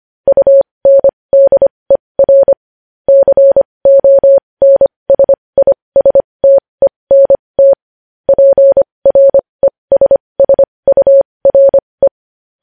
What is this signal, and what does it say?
Morse code, heard as a keyed tone, about 25 wpm, spaced as for 20 wpm, 550 Hz. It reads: UNDER CONSISTENT PRESSURE
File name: podcast / News headlines-1x-25x20